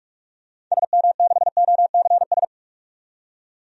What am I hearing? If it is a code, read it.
Morse code, heard as a keyed tone, about 42 wpm, 700 Hz. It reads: SM6OCS
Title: DXCC-100 Call Signs - 6 Characters - 45wpm